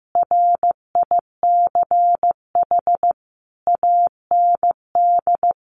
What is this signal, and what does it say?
Morse code, heard as a keyed tone, about 15 wpm, 700 Hz. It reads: RICH AND
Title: Sentences from Top 1500 Words 15wpm